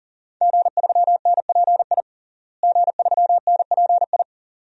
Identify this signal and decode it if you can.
Morse code, heard as a keyed tone, about 37 wpm, 700 Hz. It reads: G3NPI G3NPI
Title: DXCC-100 Call Signs - 5 Characters - Repeated 2x - 40wpm